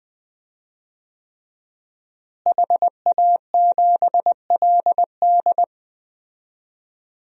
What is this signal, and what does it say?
Morse code, heard as a keyed tone, about 21 wpm, 700 Hz. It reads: HA7LD
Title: DXCC-100 Call Signs - 5 Characters - Repeated 3x - 20wpm